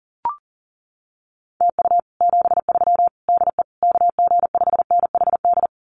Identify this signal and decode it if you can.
Morse code, heard as a keyed tone, about 40 wpm, 700 Hz. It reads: TU 73 DE KG5NHD
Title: Common QSO Phrases Encore 40wpm